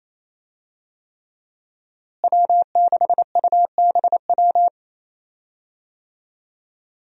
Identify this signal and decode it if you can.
Morse code, heard as a keyed tone, about 27 wpm, 700 Hz. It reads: W6UBW
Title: USA Call Signs - 5 Characters - Repeated 3x - 28wpm